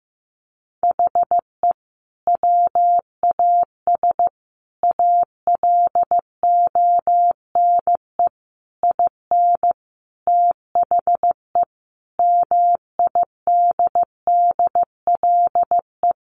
Morse code, heard as a keyed tone, about 15 wpm, 700 Hz. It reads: HE WAS ALONE IN THE MIDDLE
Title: Sentences from Top 1000 Words 15wpm